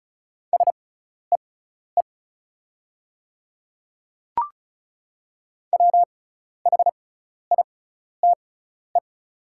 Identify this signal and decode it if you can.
Morse code, heard as a keyed tone, about 36 wpm, spaced as for 12 wpm, 690 Hz. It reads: SEE WHITE
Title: 6x Character Spacing - Top 300 Words - 35wpm